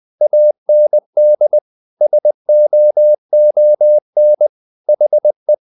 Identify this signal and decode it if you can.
Morse code, heard as a keyed tone, about 20 wpm, 600 Hz. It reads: AND SOON HE
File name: the old man and the sea part 2 20wpm